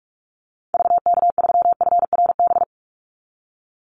Morse code, heard as a keyed tone, about 44 wpm, 700 Hz. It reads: VK3FRB
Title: USA Call Signs 45wpm